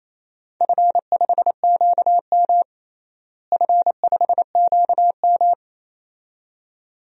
Morse code, heard as a keyed tone, about 27 wpm, 700 Hz. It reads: F5QM F5QM
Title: DXCC-100 Call Signs - 4 Characters - Repeated 2x - 28wpm